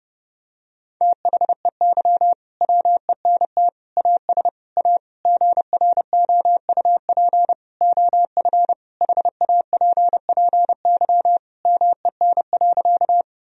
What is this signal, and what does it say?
Morse code, heard as a keyed tone, about 32 wpm, 700 Hz. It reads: THEY WENT AS A GROUP OF HAPPY MEN.
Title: Sentences from Top 1000 Words - Review Part 1 - 30wpm